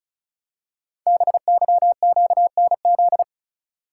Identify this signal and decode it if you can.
Morse code, heard as a keyed tone, about 36 wpm, 700 Hz. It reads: BYQNZ
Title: Code Groups - 5 Letters - 35wpm